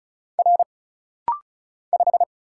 Morse code, heard as a keyed tone, about 36 wpm, 700 Hz. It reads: R 5
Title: Single Letter-Number Alt - Rapid-Fire 35wpm